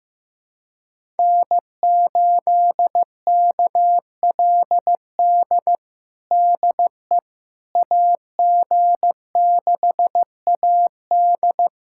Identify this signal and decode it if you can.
Morse code, heard as a keyed tone, about 15 wpm, 700 Hz. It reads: N8KLD DE AG6AD
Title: USA Callsign DE Callsign - 5 Characters - 15wpm